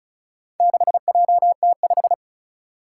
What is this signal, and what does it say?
Morse code, heard as a keyed tone, about 36 wpm, 700 Hz. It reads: 6JT5
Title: Four Challenging Letter-Number Combinations 35wpm